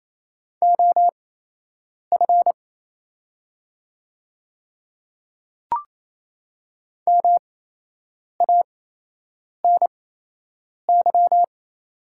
Morse code, heard as a keyed tone, about 28 wpm, spaced as for 8 wpm, 700 Hz. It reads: OF MANY
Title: 8x Character Spacing - Top 100 Words - 28wpm